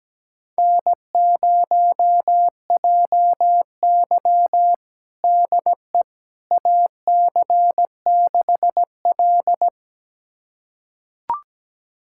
Morse code, heard as a keyed tone, about 17 wpm, 700 Hz. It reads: N0JY DE AC6L
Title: USA Callsign DE Callsign - 4 Characters - 17wpm